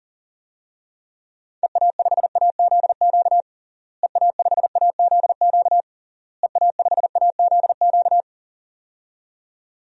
Morse code, heard as a keyed tone, about 40 wpm, 700 Hz. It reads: EA5AZQ EA5AZQ EA5AZQ
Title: DXCC-100 Call Signs - 6 Characters - Repeated 3x - 40wpm